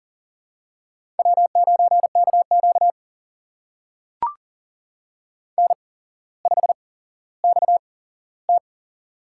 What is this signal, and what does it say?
Morse code, heard as a keyed tone, about 42 wpm, spaced as for 17 wpm, 700 Hz. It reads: W9KQ N5XT
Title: USA Call Signs - 8x Character Spacing - 40wpm